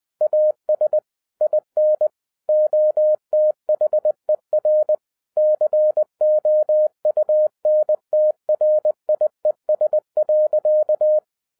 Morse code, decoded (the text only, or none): AS IN OTHER COUNTRIES.